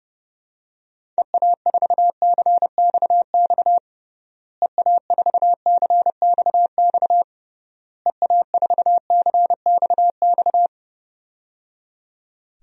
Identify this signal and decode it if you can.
Morse code, heard as a keyed tone, about 30 wpm, 700 Hz. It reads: EA4CXX EA4CXX EA4CXX
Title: DXCC-100 Call Signs - 6 Characters - Repeated 3x - 30wpm